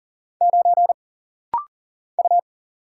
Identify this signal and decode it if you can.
Morse code, heard as a keyed tone, about 40 wpm, 700 Hz. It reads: 9 U
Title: Single Letter-Number - Rapid-Fire 40wpm